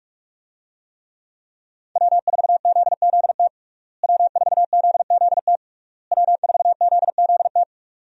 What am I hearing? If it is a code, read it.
Morse code, heard as a keyed tone, about 46 wpm, 700 Hz. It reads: W4ZZT W4ZZT W4ZZT
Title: USA Call Signs - 5 Characters - Repeated 3x - 45wpm